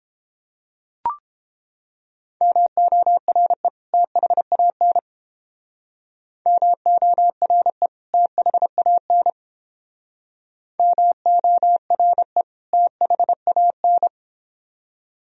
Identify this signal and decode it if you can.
Morse code, heard as a keyed tone, about 31 wpm, 700 Hz. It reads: MORE THAN MORE THAN MORE THAN